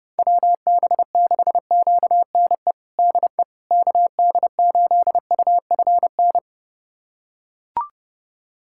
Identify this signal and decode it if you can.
Morse code, heard as a keyed tone, about 32 wpm, 700 Hz. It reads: WB6QNE DE KD8UFN
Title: USA Callsign DE Callsign - 6 Characters - 30wpm